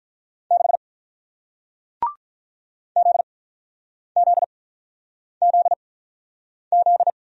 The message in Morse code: 6 Z Z Z Z

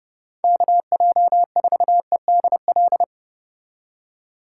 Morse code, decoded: KJ4EDL

